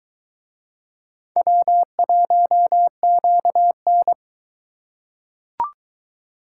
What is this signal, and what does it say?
Morse code, heard as a keyed tone, about 21 wpm, 700 Hz. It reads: W1QN